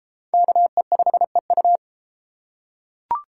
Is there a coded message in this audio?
Yes: Morse code from 0.3 to 1.8 s, heard as a keyed tone, about 31 wpm, 700 Hz. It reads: KE5EU